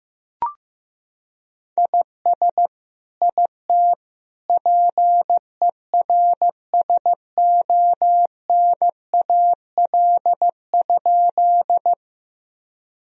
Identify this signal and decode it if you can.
Morse code, heard as a keyed tone, about 15 wpm, 700 Hz. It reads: IS IT PERSONAL?
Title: Sentences from Top 1000 Words - Review Part 4 - 15wpm